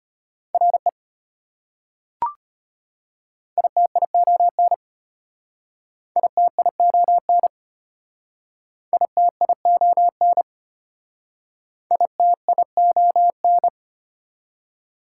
Morse code, heard as a keyed tone, about 32 wpm, 700 Hz. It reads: RE ITION ITION ITION ITION